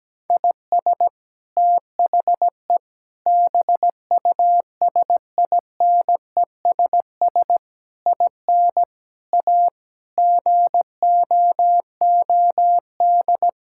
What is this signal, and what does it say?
Morse code, heard as a keyed tone, about 17 wpm, 700 Hz. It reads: IS THE BUSINESS IN A GOOD
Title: Sentences from Top 1000 Words - Review Part 5 - 17wpm